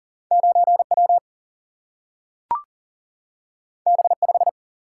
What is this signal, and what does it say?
Morse code, heard as a keyed tone, about 40 wpm, 700 Hz. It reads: 9 W B 5
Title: Single-Letter Single-Number 40wpm